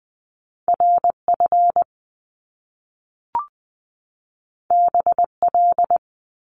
Morse code, heard as a keyed tone, about 20 wpm, 700 Hz. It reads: RF BL
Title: Top Two Letter Combinations in Words 20wpm